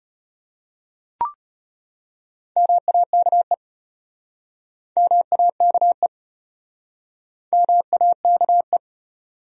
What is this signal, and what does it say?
Morse code, heard as a keyed tone, about 35 wpm, 700 Hz. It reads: MAKE MAKE MAKE